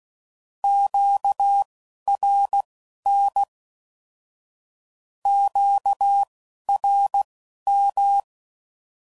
Morse code, heard as a keyed tone, about 16 wpm, spaced as for 11 wpm, 800 Hz. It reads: QRN QRM
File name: CW Code Groups @10WPM / Q Calls / Q Calls Once